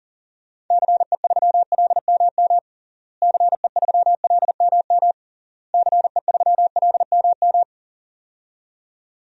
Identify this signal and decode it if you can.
Morse code, heard as a keyed tone, about 40 wpm, 700 Hz. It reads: CE3LMM CE3LMM CE3LMM
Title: DXCC-100 Call Signs - 6 Characters - Repeated 3x - 40wpm